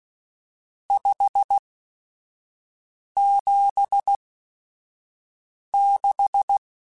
Morse code, heard as a keyed tone, about 16 wpm, spaced as for 5 wpm, 800 Hz. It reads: 576
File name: RSTs / RST-mp3s / Various RST @5